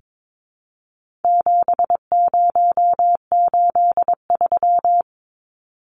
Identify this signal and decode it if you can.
Morse code, heard as a keyed tone, about 22 wpm, 700 Hz. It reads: 7083
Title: Numbers - 4-Digit 22wpm